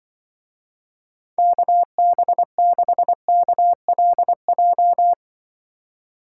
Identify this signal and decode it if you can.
Morse code, heard as a keyed tone, about 24 wpm, 700 Hz. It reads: KB6KLJ
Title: Speed Racing - USA 2x3 Call Signs 20wpm